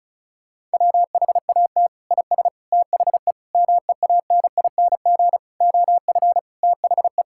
Morse code, heard as a keyed tone, about 34 wpm, 700 Hz. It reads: WHAT IS THE MEANING OF THE
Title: Sentences from Top 1600 Words 35wpm